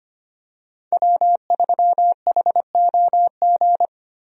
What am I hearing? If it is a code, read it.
Morse code, heard as a keyed tone, about 25 wpm, 700 Hz. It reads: W3HOG
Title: USA Call Signs - 5 Characters - Repeated 3x - 25wpm